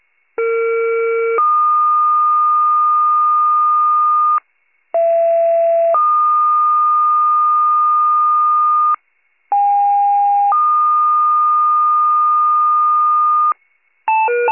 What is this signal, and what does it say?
Two-tone page, heard as a one-second tone, then a three-second tone, about 460 then 1220 Hz, about 670 then 1220 Hz, about 810 then 1220 Hz. It is heard from 0.4 to 13.5 s.
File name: pager / 39 RESCUE / wcalert